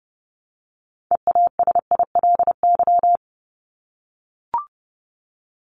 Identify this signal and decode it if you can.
Morse code, heard as a keyed tone, about 30 wpm, 700 Hz. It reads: EASILY
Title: LY Suffix - Top 100 Words with Base Words 30wpm